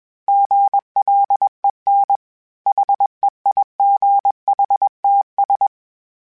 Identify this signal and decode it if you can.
Morse code, heard as a keyed tone, about 21 wpm, 800 Hz. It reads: GLEN HEIGHTS
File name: fromSMs / FromSMs mp3s / Cities@20